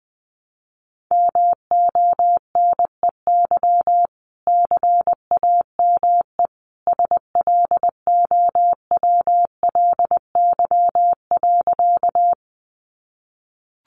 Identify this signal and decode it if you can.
Morse code, heard as a keyed tone, about 20 wpm, 700 Hz. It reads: MONEY CAME SLOWLY.
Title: Sentences from Top 1100 Words 20wpm